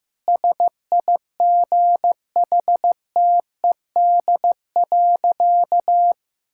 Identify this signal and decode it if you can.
Morse code, heard as a keyed tone, about 16 wpm, 700 Hz. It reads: SIGHTED.
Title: Pangram Challenge v2 - 15wpm